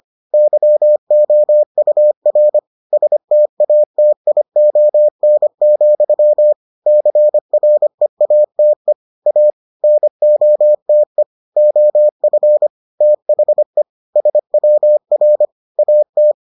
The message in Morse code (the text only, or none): YOUR STATION, CREATE A NOTE OF THE SWR AT